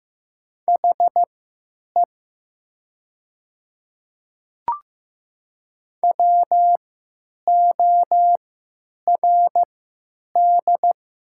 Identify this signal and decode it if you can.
Morse code, heard as a keyed tone, about 15 wpm, spaced as for 9 wpm, 700 Hz. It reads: HE WORD
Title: 3x Character Spacing - Top 100 Words - 15wpm